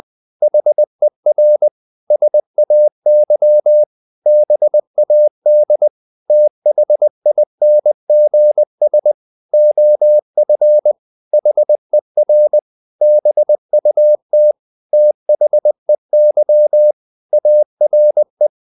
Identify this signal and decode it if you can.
Morse code, heard as a keyed tone, about 20 wpm, 600 Hz. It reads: HER SAY BAD THINGS OF HER BUT THEY ARE
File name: the old man and the sea / the old man and the sea part 1 20wpm